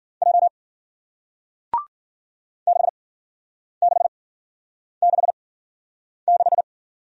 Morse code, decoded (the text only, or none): J 6666